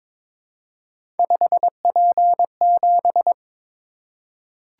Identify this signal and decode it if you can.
Morse code, heard as a keyed tone, about 22 wpm, 700 Hz. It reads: 5P7
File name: practice / 22wpm / Three Letter-Number Combinations - 22wpm